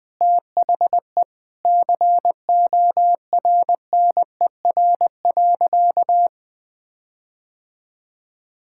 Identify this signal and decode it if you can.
Morse code, heard as a keyed tone, about 20 wpm, 700 Hz. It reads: THE CORNER.
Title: Sentences from Top 1600 Words 20wpm